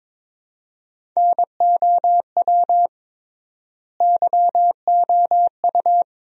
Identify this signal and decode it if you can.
Morse code, heard as a keyed tone, about 22 wpm, 700 Hz. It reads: NOW YOU
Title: 3x Word-Spacing - Sentences from Top 300 Words - 22wpm